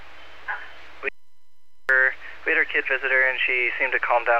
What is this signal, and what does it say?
Emergency Alert System (SAME) tones, heard as harsh buzzing bursts, with speech. It is heard from 0.4 to 3.9 s.